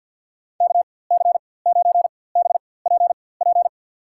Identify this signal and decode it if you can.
Morse code, heard as a keyed tone, about 53 wpm, spaced as for 24 wpm, 700 Hz. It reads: KC9BPP
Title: USA Call Signs - 6 Characters - 4x Character Spacing - 50wpm